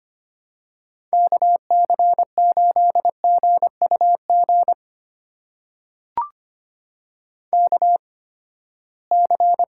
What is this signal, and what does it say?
Morse code, heard as a keyed tone, about 25 wpm, spaced as for 17 wpm, 700 Hz. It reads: KC8GUG KC
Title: USA Call Signs - 6 Characters - 8x Character Spacing - 25wpm